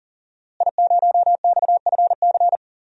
Morse code, heard as a keyed tone, about 39 wpm, 700 Hz. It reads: I0XFC
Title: DXCC-100 Call Signs - 5 Characters - Repeated 3x - 40wpm